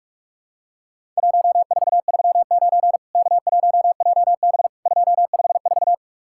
Morse code, heard as a keyed tone, about 44 wpm, 700 Hz. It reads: 1439 K1JB 254
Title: Worked All Europe DX Contest - QTC Exchange - 45wpm